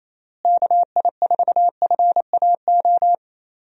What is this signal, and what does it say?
Morse code, heard as a keyed tone, about 28 wpm, 700 Hz. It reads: KI4FAO
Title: USA Call Signs - 6 Characters - Repeated 3x - 28wpm